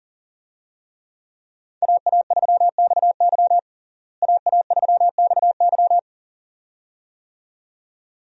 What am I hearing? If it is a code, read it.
Morse code, heard as a keyed tone, about 40 wpm, 700 Hz. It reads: AA3XY AA3XY
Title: USA Call Signs - 5 Characters - Repeated 2x - 40wpm